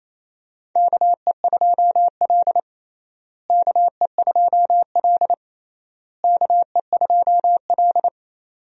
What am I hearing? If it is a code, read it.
Morse code, heard as a keyed tone, about 28 wpm, 700 Hz. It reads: KE2L KE2L KE2L
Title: USA Call Signs - 4 Characters - Repeated 3x - 28wpm